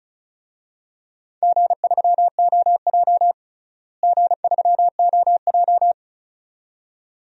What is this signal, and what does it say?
Morse code, heard as a keyed tone, about 34 wpm, 700 Hz. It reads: G3OJ G3OJ
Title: DXCC-100 Call Signs - 4 Characters - Repeated 2x - 35wpm